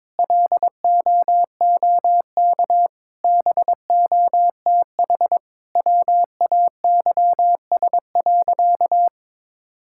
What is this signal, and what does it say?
Morse code, heard as a keyed tone, about 23 wpm, 700 Hz. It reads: LOOK BOTH WAYS.
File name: practice / 22wpm / Sentences with 2-Word Phrases - 22wpm